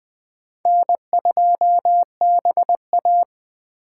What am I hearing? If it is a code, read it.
Morse code, heard as a keyed tone, about 20 wpm, 700 Hz. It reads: N2BA
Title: USA Call Signs - 4 Characters - 3x Character Spacing - 20wpm